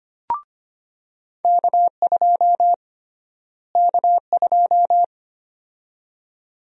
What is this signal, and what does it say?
Morse code, heard as a keyed tone, about 25 wpm, 700 Hz. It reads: K2 K2
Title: USA Call Sign Prefixes - Repeated 2x - 25wpm